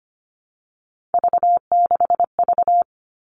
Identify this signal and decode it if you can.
Morse code, heard as a keyed tone, about 25 wpm, 700 Hz. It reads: V6V
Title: All Cut Numbers - 3-Digit 25wpm